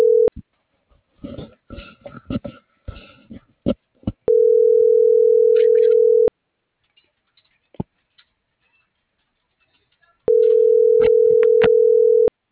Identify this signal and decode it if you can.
Ringback tone (North America), 440 + 480 Hz.